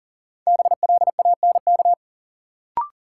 Morse code, heard as a keyed tone, about 40 wpm, 700 Hz. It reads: BLANK
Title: Sentences from Top 1500 Words - New Word Drill Alt - 40wpm